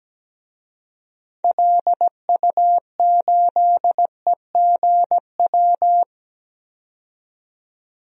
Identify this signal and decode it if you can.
Morse code, heard as a keyed tone, about 17 wpm, 700 Hz. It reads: LU8EGW
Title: DXCC-100 Call Signs - 6 Characters - 17wpm